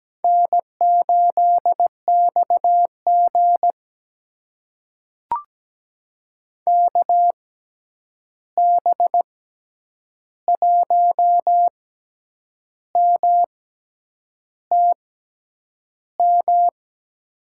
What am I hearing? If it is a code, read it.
Morse code, heard as a keyed tone, about 17 wpm, spaced as for 8 wpm, 700 Hz. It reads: N8XG KB1MTM
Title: USA Call Signs - 6x Character Spacing - 17wpm